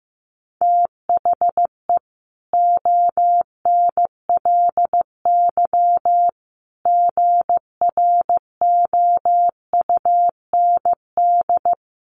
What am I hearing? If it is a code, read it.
Morse code, heard as a keyed tone, about 15 wpm, 700 Hz. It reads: THE ONLY GROUND
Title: Sentences from Top 1400 Words 15wpm